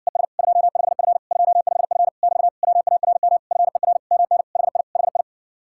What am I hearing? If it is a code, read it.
Morse code, heard as a keyed tone, about 60 wpm, 700 Hz. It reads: ES 25F 25F <BT> WARM FER MN HI HI